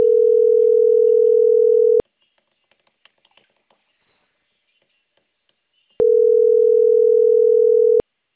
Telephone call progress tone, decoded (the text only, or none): ringback tone